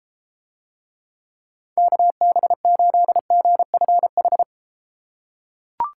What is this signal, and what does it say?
Morse code, heard as a keyed tone, about 32 wpm, 700 Hz. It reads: KB8GFH